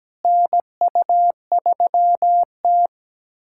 Morse code, heard as a keyed tone, about 17 wpm, 700 Hz. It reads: NU3T
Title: USA Callsign DE Callsign - 4 Characters - 17wpm